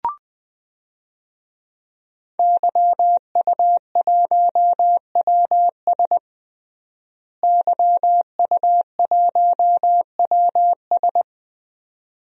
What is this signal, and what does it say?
Morse code, heard as a keyed tone, about 20 wpm, 700 Hz. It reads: YU1WS YU1WS
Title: DXCC-100 Call Signs - 5 Characters - Repeated 2x - 20wpm